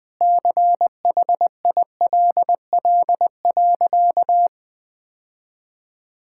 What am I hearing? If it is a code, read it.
Morse code, heard as a keyed tone, about 20 wpm, 700 Hz. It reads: CHILL.
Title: Difficult Sentences with Hints - 20wpm